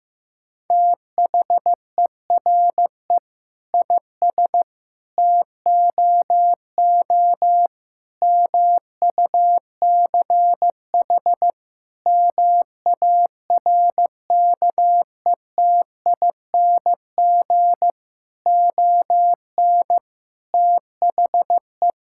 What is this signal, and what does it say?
Morse code, heard as a keyed tone, about 15 wpm, 700 Hz. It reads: THERE IS TOO MUCH MARKETING ON THE
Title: Sentences from Top 1000 Words - Review Part 5 - 15wpm